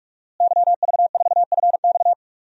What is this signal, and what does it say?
Morse code, heard as a keyed tone, about 43 wpm, 700 Hz. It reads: YV4FX
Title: DXCC-100 Call Signs - 5 Characters - Repeated 3x - 45wpm